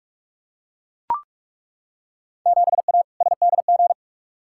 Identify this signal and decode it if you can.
Morse code, heard as a keyed tone, about 42 wpm, 700 Hz. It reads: 7A SDG
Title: Field Day Exchanges Encore - 45wpm